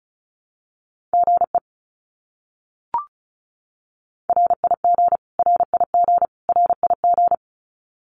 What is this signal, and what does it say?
Morse code, heard as a keyed tone, about 34 wpm, 700 Hz. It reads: GE RIG RIG RIG
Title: Abbreviations Repeated 3x - 35wpm